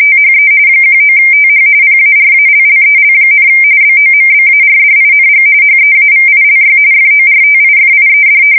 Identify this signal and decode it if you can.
RTTY, about 45.45 baud. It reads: THE 19TH PARALLEL, BOMBARDING HEAVILY FORTIFIED
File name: fsk170 / battleshipnj